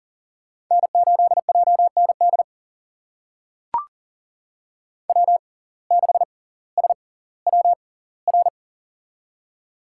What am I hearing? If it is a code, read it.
Morse code, heard as a keyed tone, about 40 wpm, spaced as for 20 wpm, 700 Hz. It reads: N8JND W6SWR
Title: USA Call Signs - 5 Characters - 6x Character Spacing - 40wpm